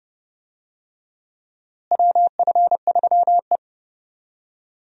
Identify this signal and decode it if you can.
Morse code, heard as a keyed tone, about 31 wpm, 700 Hz. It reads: WF3E
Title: USA Call Signs - 4 Characters - Repeated 2x - 30wpm